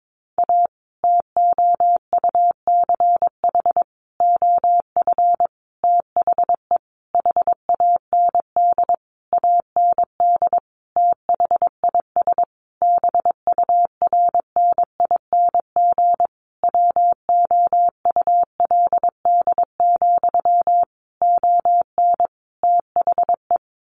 Morse code, heard as a keyed tone, about 22 wpm, 700 Hz. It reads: A TOUCH OF THE HAND AND THIS BURNING WOULD, ON THE
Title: A Sound Of Thunder 22wpm